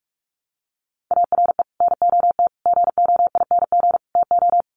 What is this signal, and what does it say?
Morse code, heard as a keyed tone, about 44 wpm, 700 Hz. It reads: ARE NOT GOING TO
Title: Sets of 4 Words 45wpm